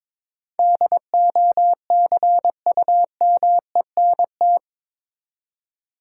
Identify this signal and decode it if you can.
Morse code, heard as a keyed tone, about 22 wpm, 700 Hz. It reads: DOCUMENT
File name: practice / 22wpm / Eight Letter Words - 22wpm